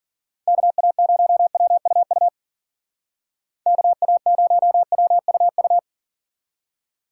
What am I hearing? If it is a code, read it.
Morse code, heard as a keyed tone, about 49 wpm, 700 Hz. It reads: KA0WUU KA0WUU
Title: Speed Racing - USA Call Signs 40wpm